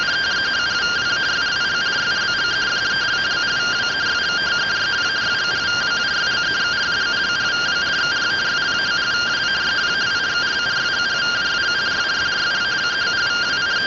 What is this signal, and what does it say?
Two-tone teleprinter signal, about 170 Hz shift.
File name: findings / CT AXIAL -256.gen-22 rtty rx